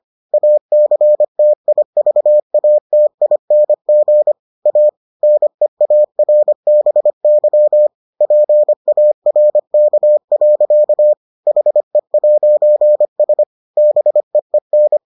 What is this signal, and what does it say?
Morse code, heard as a keyed tone, about 25 wpm, 600 Hz. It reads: ACTIVATING A NEARBY PARK. HE'S BEEN